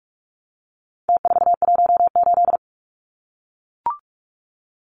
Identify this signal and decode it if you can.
Morse code, heard as a keyed tone, about 44 wpm, 700 Hz. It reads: T418
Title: Numbers - Contest Style - 4-Digit 45wpm